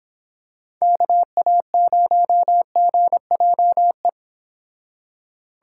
Morse code, heard as a keyed tone, about 26 wpm, 700 Hz. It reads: KA0GJE